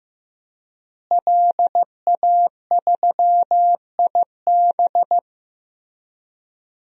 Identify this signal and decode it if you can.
Morse code, heard as a keyed tone, about 15 wpm, 700 Hz. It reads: LA3IB
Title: DXCC-100 Call Signs - 5 Characters - Repeated 3x - 15wpm